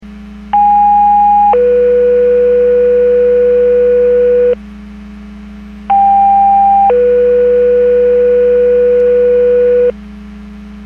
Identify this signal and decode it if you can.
Two-tone page, heard as a one-second tone, then a three-second tone, about 820 then 490 Hz, about 800 then 490 Hz. It is heard from 0.5 to 9.9 s.
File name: Portage / 02601 16-37 Portage 154.385 Paging